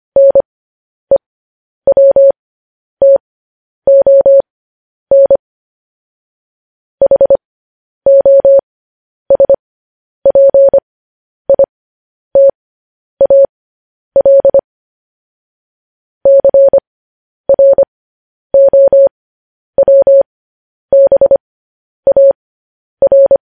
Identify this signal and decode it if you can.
Morse code, heard as a keyed tone, about 25 wpm, spaced as for 10 wpm, 550 Hz. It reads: NEWTON HOSPITAL CROWBAR
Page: News Headlines in Morse Code at 10 WPM